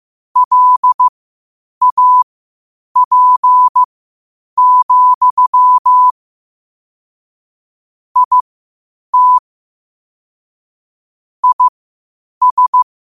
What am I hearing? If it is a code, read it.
Morse code, heard as a keyed tone, about 15 wpm, spaced as for 8 wpm, 1000 Hz. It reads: LAP, IT IS
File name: qotdfiles / 8wpm